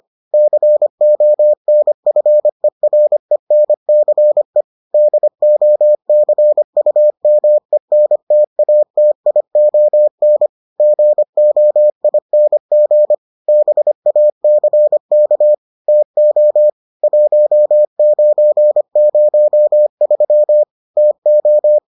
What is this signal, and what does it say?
Morse code, heard as a keyed tone, about 25 wpm, 600 Hz. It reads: CONFERENCE DOCUMENTATION GOING BACK TO 1903 TO